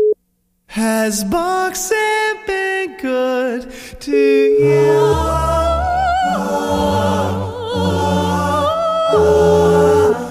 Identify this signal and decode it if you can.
Ringback tone (Europe, 425 Hz), with noise.